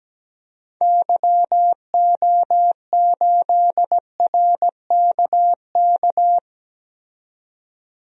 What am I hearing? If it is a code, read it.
Morse code, heard as a keyed tone, about 17 wpm, 700 Hz. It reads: YO8RKK
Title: DXCC-100 Call Signs - 6 Characters - 17wpm